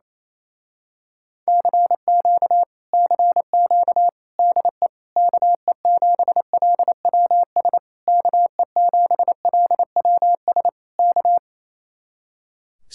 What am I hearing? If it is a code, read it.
Morse code, heard as a keyed tone, about 27 wpm, 700 Hz. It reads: CQ CQ DE KE7LWS KE7LWS K
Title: Speed-Racing Common QSO Phrases Encore - 28wpm